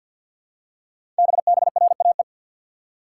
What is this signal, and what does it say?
Morse code, heard as a keyed tone, about 48 wpm, 690 Hz. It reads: BBRAE
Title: Code Groups - 5 Letters - 50wpm